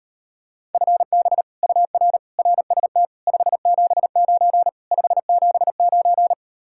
Morse code, heard as a keyed tone, about 38 wpm, 700 Hz. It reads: FB UR RST 579 579